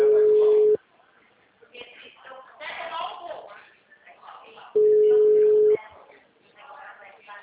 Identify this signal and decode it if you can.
Ringback tone (Europe, 425 Hz), with speech.
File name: out- -2015- - - .8614